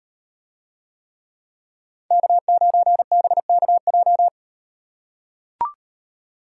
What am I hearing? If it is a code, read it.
Morse code, heard as a keyed tone, about 36 wpm, 700 Hz. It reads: K9BKJ